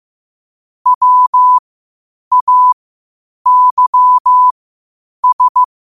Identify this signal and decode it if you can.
Morse code, heard as a keyed tone, about 15 wpm, spaced as for 8 wpm, 1000 Hz. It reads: WAYS
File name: qotdfiles / 8wpm